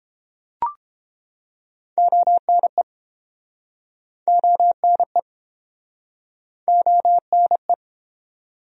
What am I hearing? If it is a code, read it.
Morse code, heard as a keyed tone, about 31 wpm, 700 Hz. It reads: ONE ONE ONE